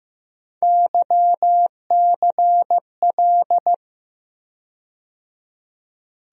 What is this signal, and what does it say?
Morse code, heard as a keyed tone, about 15 wpm, 700 Hz. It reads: YCL
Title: USA Call Sign Postfixes - 3 Letters 15wpm